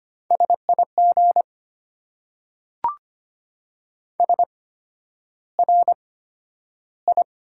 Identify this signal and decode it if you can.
Morse code, heard as a keyed tone, about 25 wpm, spaced as for 11 wpm, 690 Hz. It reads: SIG SRI
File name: Top 90 QSO Elements - 8x Character Spacing - 25wpm